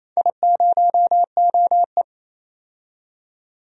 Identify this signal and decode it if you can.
Morse code, heard as a keyed tone, about 28 wpm, 700 Hz. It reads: I0OE
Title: DXCC-100 Call Signs - 4 Characters - 28wpm